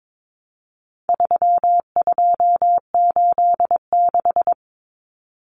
Morse code, heard as a keyed tone, about 22 wpm, 700 Hz. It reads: 3286
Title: Common Cut Numbers - 4-Digit 22wpm